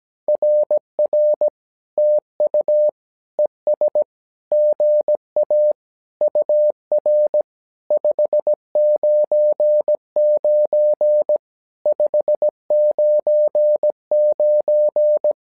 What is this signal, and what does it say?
Morse code, heard as a keyed tone, about 17 wpm, 600 Hz. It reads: RR TU ES GA UR 599 599
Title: SOTA QSO Practice 17wpm